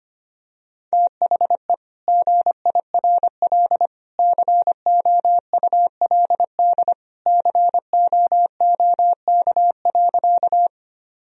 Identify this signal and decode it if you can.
Morse code, heard as a keyed tone, about 25 wpm, 700 Hz. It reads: THE GIRL COULD COOK.
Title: Sentences from Top 1000 Words - Review Part 4 - 25wpm